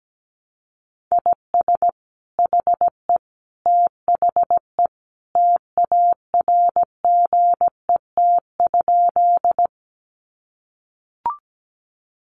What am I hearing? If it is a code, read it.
Morse code, heard as a keyed tone, about 17 wpm, 700 Hz. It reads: IS HE THE TARGET?
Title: Sentences from Top 1200 Words 17wpm